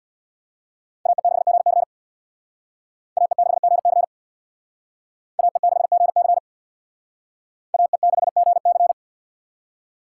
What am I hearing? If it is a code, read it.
Morse code, heard as a keyed tone, about 59 wpm, 700 Hz. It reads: AE6GC AE6GC AE6GC AE6GC
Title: Speed Racing - USA Call Signs 50wpm